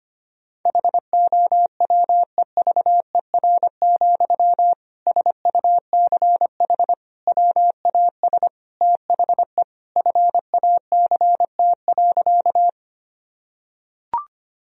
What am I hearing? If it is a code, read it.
Morse code, heard as a keyed tone, about 25 wpm, 700 Hz. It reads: HOWEVER, SUCH WAS THE FACT.
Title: Sentences from Top 1000 Words - Review Part 3 - 25wpm